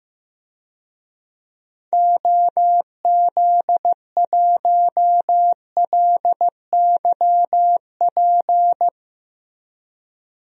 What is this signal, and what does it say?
Morse code, heard as a keyed tone, about 15 wpm, 700 Hz. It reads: OZ1LYP
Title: DXCC-100 Call Signs - 6 Characters - Repeated 3x - 15wpm